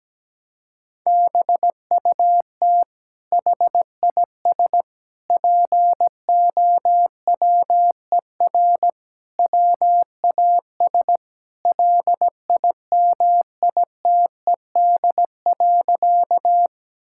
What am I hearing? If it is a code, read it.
Morse code, heard as a keyed tone, about 17 wpm, 700 Hz. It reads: BUT HIS POWER WAS LIMITED.
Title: Sentences from Top 1000 Words - Review Part 6 - 17wpm